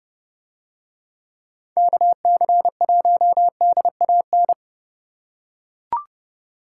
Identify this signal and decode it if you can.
Morse code, heard as a keyed tone, about 29 wpm, 700 Hz. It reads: KC1DAN